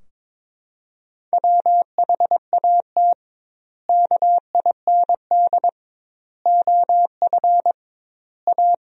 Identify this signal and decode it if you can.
Morse code, heard as a keyed tone, about 22 wpm, 700 Hz. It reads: WHAT KIND OF A
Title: 2x Word-Spacing - Sentences from Top 400 Words - 22wpm